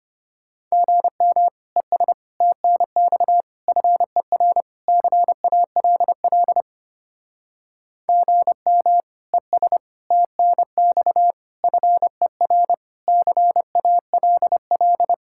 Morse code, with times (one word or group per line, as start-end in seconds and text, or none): 0.7-1.5 GM
1.8-2.1 ES
2.4-3.4 TNX
3.7-4.6 FER
4.9-6.6 CALL
8.1-9.0 GM
9.3-9.8 ES
10.1-11.3 TNX
11.6-12.7 FER
13.1-15.1 CALL